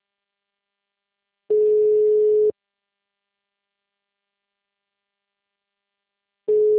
Ringback tone (Europe, 425 Hz).
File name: out- -2030- - - .14